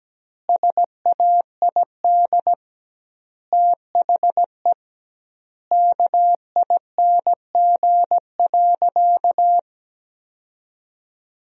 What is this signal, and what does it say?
Morse code, heard as a keyed tone, about 17 wpm, 700 Hz. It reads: SAID THE KING.
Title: 2x Word-Spacing - Sentences from Top 400 Words - 17wpm